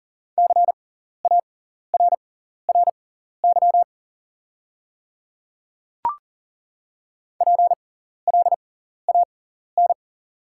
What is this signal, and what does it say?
Morse code, heard as a keyed tone, about 40 wpm, spaced as for 14 wpm, 690 Hz. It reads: CARRY PLAN
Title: 6x Character Spacing - Top 500 Words - 40wpm